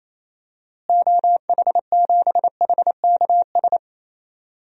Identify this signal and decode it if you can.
Morse code, heard as a keyed tone, about 28 wpm, 700 Hz. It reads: OH7HKS
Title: DXCC-100 Call Signs - 6 Characters - Repeated 3x - 28wpm